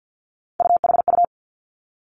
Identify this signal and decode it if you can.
Morse code, heard as a keyed tone, about 50 wpm, 700 Hz. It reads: UHU
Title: Three Challenging Letter-Number Combinations 50wpm